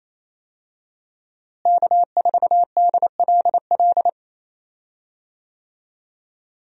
Morse code, heard as a keyed tone, about 28 wpm, 700 Hz. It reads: K4DLL
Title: USA Call Signs - 5 Characters - Repeated 2x - 28wpm